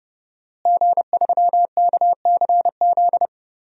Morse code, heard as a keyed tone, about 30 wpm, 700 Hz. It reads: G3KCZ
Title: DXCC-100 Call Signs - 5 Characters - Repeated 2x - 30wpm